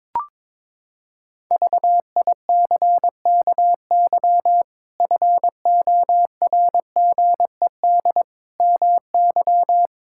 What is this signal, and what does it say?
Morse code, heard as a keyed tone, about 22 wpm, 700 Hz. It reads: VICKY FORGED MY
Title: Pangram Challenge v2 - 22wpm